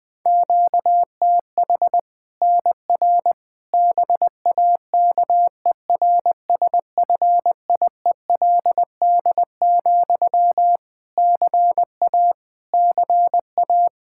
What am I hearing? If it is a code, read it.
Morse code, heard as a keyed tone, about 20 wpm, 700 Hz. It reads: QTH NR BAKERSFIELD, CA CA
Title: Practice QSO - Exchange 1 - Beginning - 20wpm